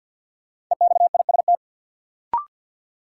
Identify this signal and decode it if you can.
Morse code, heard as a keyed tone, about 53 wpm, 700 Hz. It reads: EXIST
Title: Sentences from Top 1000 Words - Review Part 5 - New Word Drill - 50wpm